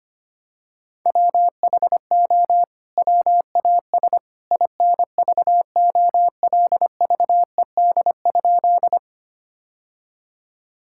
Morse code, heard as a keyed tone, about 25 wpm, 700 Hz. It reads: WHO WAS INVOLVED?
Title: Sentences from Top 1000 Words - Review Part 5 - 25wpm